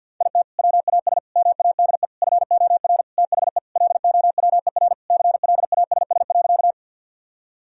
Morse code, heard as a keyed tone, about 50 wpm, 700 Hz. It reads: IT WAS MADE FOR THE LOWER CLASS.
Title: Sentences from Top 1000 Words - Review Part 4 - 50wpm